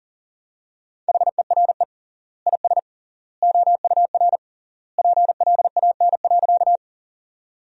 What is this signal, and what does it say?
Morse code, heard as a keyed tone, about 40 wpm, 700 Hz. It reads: HERE IS OUR PLAN.
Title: 3x Word-Spacing - Sentences from Top 500 Words - 40wpm